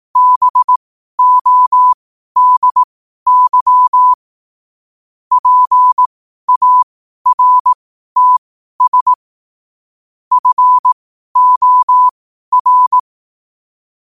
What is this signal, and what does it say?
Morse code, heard as a keyed tone, about 18 wpm, spaced as for 12 wpm, 1000 Hz. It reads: BODY PARTS FOR
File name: qotdfiles / 12wpm